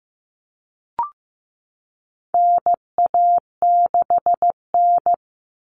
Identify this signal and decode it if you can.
Morse code, heard as a keyed tone, about 15 wpm, 700 Hz. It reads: NA6N
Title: Common Cut Numbers - 4-Digit 15wpm